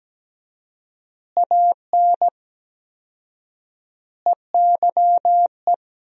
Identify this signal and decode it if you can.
Morse code, heard as a keyed tone, about 17 wpm, 700 Hz. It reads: AN EYE
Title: 4x Word-Spacing - Sentences from Top 300 Words - 17wpm